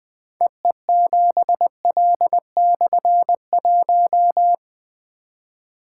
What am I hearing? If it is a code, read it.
Morse code, heard as a keyed tone, about 20 wpm, 700 Hz. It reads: EE7L/1
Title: Difficult Call Signs 20wpm